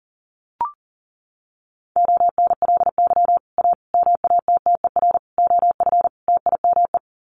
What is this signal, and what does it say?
Morse code, heard as a keyed tone, about 40 wpm, 700 Hz. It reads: ONLY A MATTER OF TIME
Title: Sets of 5 Words Encore 40wpm